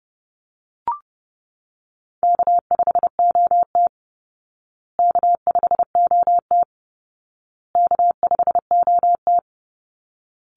Morse code, heard as a keyed tone, about 30 wpm, 700 Hz. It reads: K5OT K5OT K5OT
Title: Call Signs Repeated 2x - 30wpm